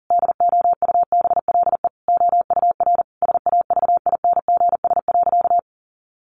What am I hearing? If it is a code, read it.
Morse code, heard as a keyed tone, about 40 wpm, 700 Hz. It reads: DOUBLE OUR SAVINGS.
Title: Sentences from Top 1400 Words 40wpm